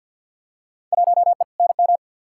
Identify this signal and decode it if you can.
Morse code, heard as a keyed tone, about 47 wpm, 700 Hz. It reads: 1E NM
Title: Field Day Exchanges - 50wpm